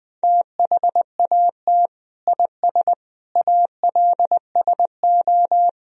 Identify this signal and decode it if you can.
Morse code, heard as a keyed tone, about 20 wpm, 700 Hz. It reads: THAT IS ALSO
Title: Sentences from Top 1000 Words - Review Part 1 - 20wpm